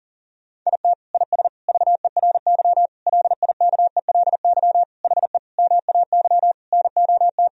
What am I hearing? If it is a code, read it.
Morse code, heard as a keyed tone, about 37 wpm, 700 Hz. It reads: IT IS VERY LIKELY HE MAY NOT
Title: Sentences from Top 1000 Words - Review Part 4 - 40wpm